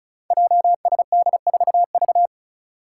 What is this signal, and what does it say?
Morse code, heard as a keyed tone, about 36 wpm, 700 Hz. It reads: JSD4V
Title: Five Challenging Letter-Number Combinations 35wpm